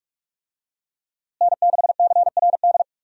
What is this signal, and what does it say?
Morse code, heard as a keyed tone, about 41 wpm, 700 Hz. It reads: N6CRD
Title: USA Call Signs - 5 Characters - Repeated 3x - 45wpm